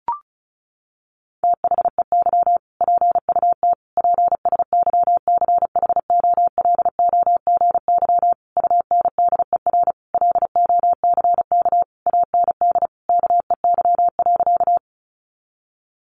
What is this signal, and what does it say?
Morse code, heard as a keyed tone, about 34 wpm, 700 Hz. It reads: THEY PUT PSYCHOLOGY UNDER LOCK AND KEY.
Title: Sentences from Top 1400 Words 35wpm